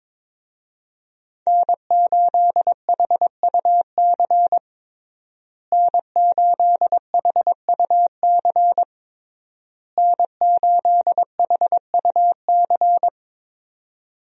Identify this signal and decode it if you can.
Morse code, heard as a keyed tone, about 22 wpm, 700 Hz. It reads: N8HUC N8HUC N8HUC
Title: USA Call Signs - 5 Characters - Repeated 3x - 22wpm